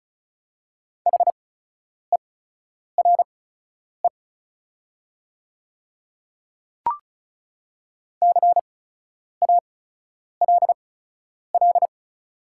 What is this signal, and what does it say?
Morse code, heard as a keyed tone, about 36 wpm, spaced as for 10 wpm, 690 Hz. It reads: HERE CALL
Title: 8x Character Spacing - Top 200 Words - 35wpm